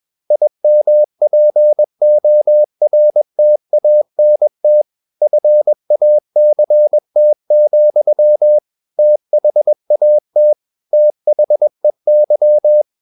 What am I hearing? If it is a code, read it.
Morse code, heard as a keyed tone, about 21 wpm, 600 Hz. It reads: IMPORTANT FACT, THAT THEY
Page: CW med Gnister